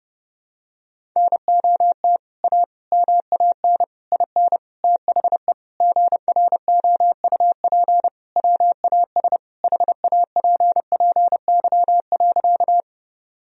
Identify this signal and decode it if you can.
Morse code, heard as a keyed tone, about 30 wpm, 700 Hz. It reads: NOT A MAN IN THE GROUP WAS HAPPY.
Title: Sentences from Top 1000 Words - Review Part 1 - 30wpm